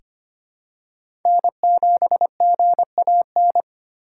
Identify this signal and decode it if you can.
Morse code, heard as a keyed tone, about 25 wpm, 700 Hz. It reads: N7GAN
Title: USA Call Signs - 6x Character Spacing - 25wpm